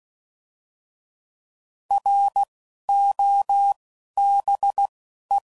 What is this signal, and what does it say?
Morse code, heard as a keyed tone, about 16 wpm, 800 Hz. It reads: R O B E
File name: Prefixes / By First Letter / Rs